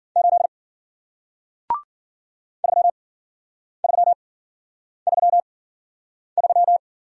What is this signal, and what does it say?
Morse code, heard as a keyed tone, about 55 wpm, 700 Hz. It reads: 8 3 3 3 3